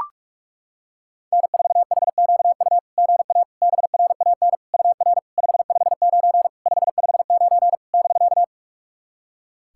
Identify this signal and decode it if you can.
Morse code, heard as a keyed tone, about 42 wpm, 660 Hz. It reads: N4HQU GA BRAN UR 559 559 <BK>
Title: SOTA QSO Practice 45wpm